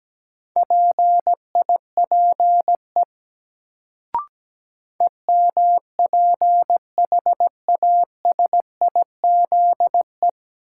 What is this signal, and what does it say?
Morse code, heard as a keyed tone, about 17 wpm, 700 Hz. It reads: PIPE EMPHASIZE
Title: Sentences from Top 1500 Words - New Word Drill - 17wpm